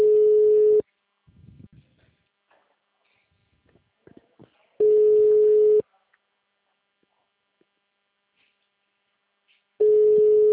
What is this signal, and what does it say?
Ringback tone (Europe, 425 Hz).